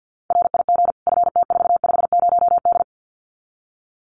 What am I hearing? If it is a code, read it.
Morse code, heard as a keyed tone, about 50 wpm, 700 Hz. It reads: RIG FT450D
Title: Common QSO Phrases 50wpm